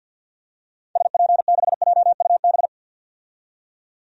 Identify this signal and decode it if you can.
Morse code, heard as a keyed tone, about 52 wpm, 700 Hz. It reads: SP6JUB
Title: DXCC-100 Call Signs - 6 Characters - 50wpm